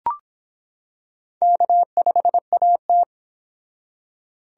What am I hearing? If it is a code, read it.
Morse code, heard as a keyed tone, about 24 wpm, 700 Hz. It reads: K5AT